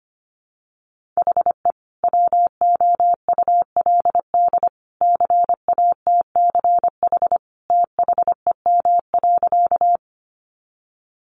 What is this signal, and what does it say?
Morse code, heard as a keyed tone, about 25 wpm, 700 Hz. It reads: HE WOULD CATCH THEM.
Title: Sentences from Top 1500 Words 25wpm